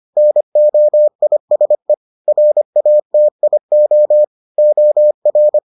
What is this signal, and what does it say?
Morse code, heard as a keyed tone, about 25 wpm, 600 Hz. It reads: NOISE RATIO OR